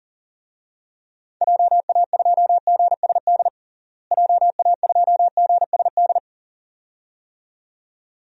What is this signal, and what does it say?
Morse code, heard as a keyed tone, about 40 wpm, 700 Hz. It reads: JA2GSD JA2GSD
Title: DXCC-100 Call Signs - 6 Characters - Repeated 2x - 40wpm